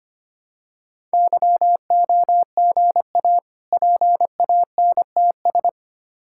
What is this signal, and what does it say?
Morse code, heard as a keyed tone, about 25 wpm, 700 Hz. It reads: YOGA PANTS
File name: practice / 25wpm / Difficult Words with Hints - 25wpm